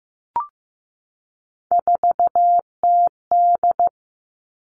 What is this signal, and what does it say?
Morse code, heard as a keyed tone, about 15 wpm, 700 Hz. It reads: 4TD
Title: All Cut Numbers - 3-Digit 15wpm